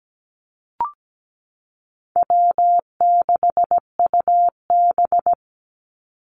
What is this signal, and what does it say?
Morse code, heard as a keyed tone, about 17 wpm, 700 Hz. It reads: W6UB
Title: NA SOTA Activator Callsigns 17wpm